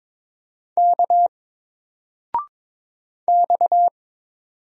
Morse code, heard as a keyed tone, about 22 wpm, 700 Hz. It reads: K X
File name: Single-Letter-Number-22wpm